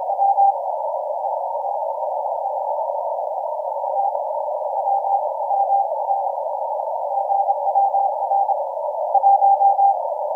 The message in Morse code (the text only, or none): none